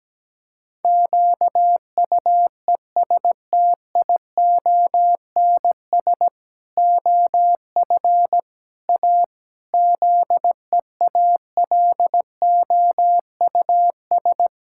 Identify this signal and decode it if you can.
Morse code, heard as a keyed tone, about 17 wpm, 700 Hz. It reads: QUESTIONS OF A ZEALOUS
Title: Pangram Challenge v2 - 17wpm